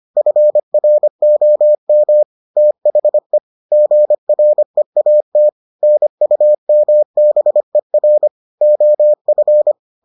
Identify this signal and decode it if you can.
Morse code, heard as a keyed tone, about 25 wpm, 600 Hz. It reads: FROM THE GREAT NUMBER OF